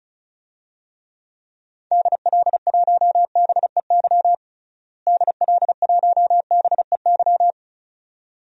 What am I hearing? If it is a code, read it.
Morse code, heard as a keyed tone, about 34 wpm, 700 Hz. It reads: DL1BEY DL1BEY
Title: DXCC-100 Call Signs - 6 Characters - Repeated 2x - 35wpm